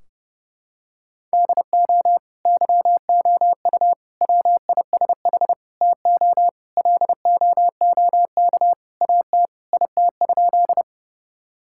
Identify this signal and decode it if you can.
Morse code, heard as a keyed tone, about 30 wpm, 700 Hz. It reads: DO YOU WISH TO LOOK AT IT?
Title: Sentences from Top 1600 Words 30wpm